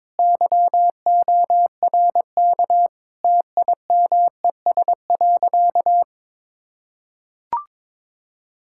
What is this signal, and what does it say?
Morse code, heard as a keyed tone, about 22 wpm, 700 Hz. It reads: YORK TIMES.
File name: practice / 22wpm / Sentences with 4-Word Phrases - 22wpm